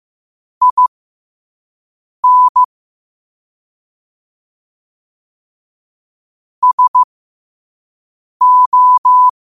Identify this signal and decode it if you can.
Morse code, heard as a keyed tone, about 15 wpm, spaced as for 5 wpm, 1000 Hz. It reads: IN SO
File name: qotdfiles / 5wpm